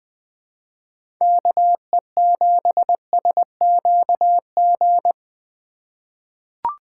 Morse code, heard as a keyed tone, about 20 wpm, 700 Hz. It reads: KE7SQG
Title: USA Call Signs - 6 Characters - 4x Character Spacing - 20wpm